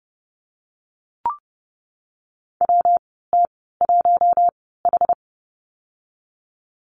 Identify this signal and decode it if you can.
Morse code, heard as a keyed tone, about 30 wpm, spaced as for 17 wpm, 700 Hz. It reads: WT1H
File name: Lesson 19g - Letter w - Call Signs 30wpm